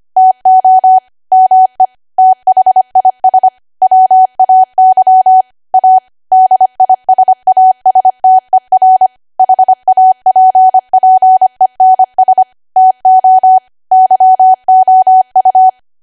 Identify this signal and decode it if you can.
Morse code, heard as a keyed tone, about 25 wpm, 750 Hz. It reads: TO ME THIS WAY A DISASTER HAPPENS TO YOU
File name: Archive / 25 WPM / 25WPM